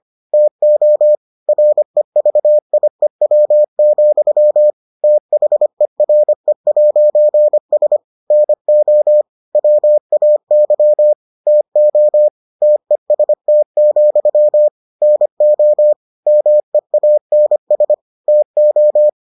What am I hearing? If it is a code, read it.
Morse code, heard as a keyed tone, about 25 wpm, 600 Hz. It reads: TO REVIEW, THERE'S NO WAY TO TEST, NO MEANS TO